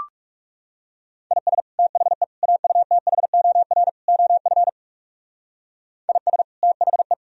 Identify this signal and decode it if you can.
Morse code, heard as a keyed tone, about 39 wpm, 700 Hz. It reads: IS THE AUTHOR OF IS THE